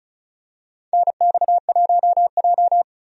Morse code, heard as a keyed tone, about 36 wpm, 700 Hz. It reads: NX1J
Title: USA Call Signs - 4 Characters - 35wpm